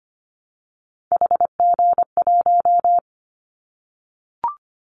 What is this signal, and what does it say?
Morse code, heard as a keyed tone, about 25 wpm, 700 Hz. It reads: HG1
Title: Callsign Prefixes for Top 50 DXCC - 25wpm